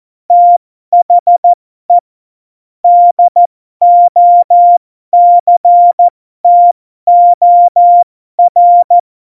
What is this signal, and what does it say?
Morse code, heard as a keyed tone, about 14 wpm, spaced as for 12 wpm, 700 Hz. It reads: THE DOCTOR
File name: Morsekurs / Witze